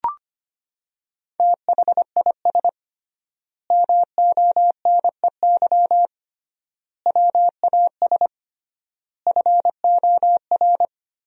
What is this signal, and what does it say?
Morse code, heard as a keyed tone, about 25 wpm, 700 Hz. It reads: THIS MONEY WAS FOR
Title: 3x Word-Spacing - Sentences from Top 400 Words - 25wpm